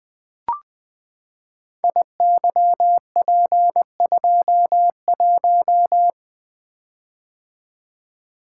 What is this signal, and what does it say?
Morse code, heard as a keyed tone, about 20 wpm, 700 Hz. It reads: IYP21
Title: Code Groups - 5 Letter:Numbers - 20wpm